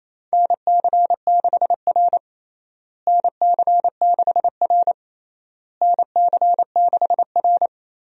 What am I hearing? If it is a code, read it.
Morse code, heard as a keyed tone, about 27 wpm, 700 Hz. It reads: NC6R NC6R NC6R
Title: USA Call Signs - 4 Characters - Repeated 3x - 28wpm